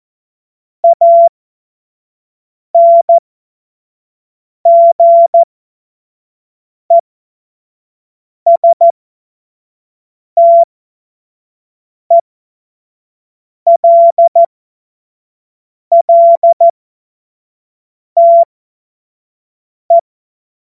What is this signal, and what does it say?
Morse code, heard as a keyed tone, about 14 wpm, spaced as for 5 wpm, 680 Hz. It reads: ANGESTELLTE
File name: Morsekurs / Witze / Witz 27. Dezember 2022 25 F70